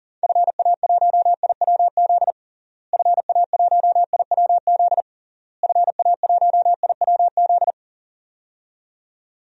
Morse code, heard as a keyed tone, about 40 wpm, 700 Hz. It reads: FA1IWZ FA1IWZ FA1IWZ
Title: DXCC-100 Call Signs - 6 Characters - Repeated 3x - 40wpm